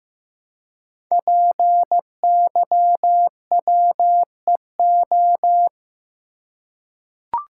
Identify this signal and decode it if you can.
Morse code, heard as a keyed tone, about 15 wpm, 700 Hz. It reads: PYWEO
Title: Code Groups - 5 Letters - 15wpm